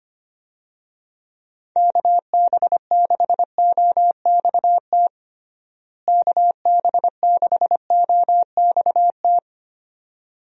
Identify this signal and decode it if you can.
Morse code, heard as a keyed tone, about 25 wpm, 700 Hz. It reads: KB6OXT KB6OXT
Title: USA Call Signs - 6 Characters - Repeated 2x - 25wpm